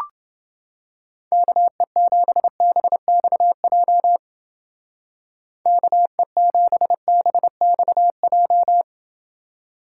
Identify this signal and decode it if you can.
Morse code, heard as a keyed tone, about 29 wpm, 700 Hz. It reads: KE7BXJ KE7BXJ